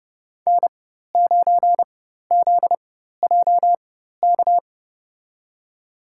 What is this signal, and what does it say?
Morse code, heard as a keyed tone, about 32 wpm, spaced as for 14 wpm, 700 Hz. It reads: N9ZJK
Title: USA Call Signs - 5 Characters - 4x Character Spacing - 30wpm